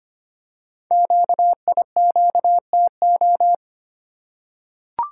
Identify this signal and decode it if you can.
Morse code, heard as a keyed tone, about 25 wpm, 700 Hz. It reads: QIQTO
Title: Code Groups - 5 Letters - 25wpm